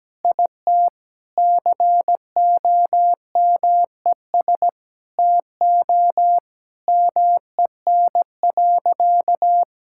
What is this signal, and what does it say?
Morse code, heard as a keyed tone, about 17 wpm, 700 Hz. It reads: IT COMES TO MEN.
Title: Sentences from Top 1000 Words - Review Part 6 - 17wpm